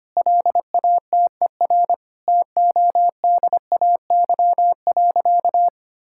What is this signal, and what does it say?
Morse code, heard as a keyed tone, about 25 wpm, 700 Hz. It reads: LATER TODAY.
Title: Sentences from Top 1000 Words - Review Part 6 - 25wpm